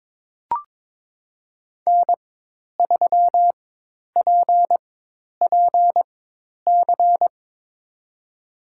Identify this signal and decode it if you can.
Morse code, heard as a keyed tone, about 22 wpm, spaced as for 10 wpm, 700 Hz. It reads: N3PPC
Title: USA Call Signs - 5 Characters - 4x Character Spacing - 22wpm